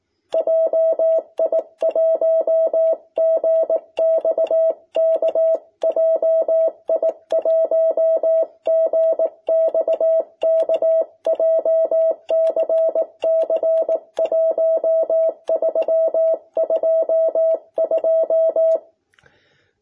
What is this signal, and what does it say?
Morse code, heard as a keyed tone, about 18 wpm, 650 Hz. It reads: JI1GXK JI1GXK JCC 1322
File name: jh8cbh / 2021cw